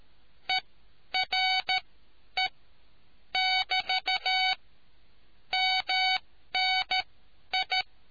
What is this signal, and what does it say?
Morse code, heard as a keyed tone, about 13 wpm, spaced as for 9 wpm, 750 Hz. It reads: ERE<BT> MNI